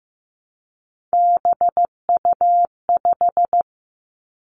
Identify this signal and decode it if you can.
Morse code, heard as a keyed tone, about 15 wpm, 700 Hz. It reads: BU5
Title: Three Challenging Letter-Number Combinations 15wpm